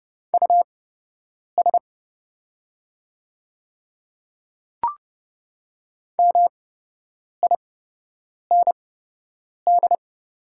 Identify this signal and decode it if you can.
Morse code, heard as a keyed tone, about 30 wpm, spaced as for 8 wpm, 690 Hz. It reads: US MIND
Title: 8x Character Spacing - Top 500 Words - 30wpm